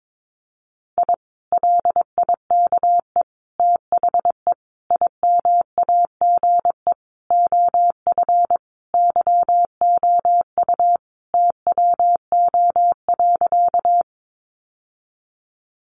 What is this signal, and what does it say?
Morse code, heard as a keyed tone, about 22 wpm, 700 Hz. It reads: I LIKE THE IMAGE OF YOU TWO.
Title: Sentences from Top 700 Words 22wpm